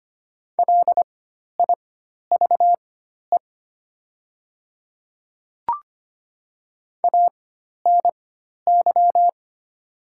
Morse code, heard as a keyed tone, about 25 wpm, spaced as for 12 wpm, 700 Hz. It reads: LIVE ANY
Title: 4x Character Spacing - Top 200 Words - 25wpm